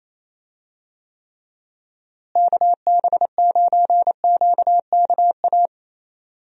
Morse code, heard as a keyed tone, about 28 wpm, 700 Hz. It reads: KB9QKA
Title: USA Call Signs - 6 Characters - Repeated 2x - 28wpm